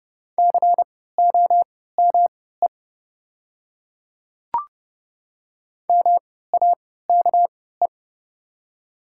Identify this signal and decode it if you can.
Morse code, heard as a keyed tone, about 30 wpm, spaced as for 17 wpm, 700 Hz. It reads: COME MAKE
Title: 3x Character Spacing - Top 100 Words - 30wpm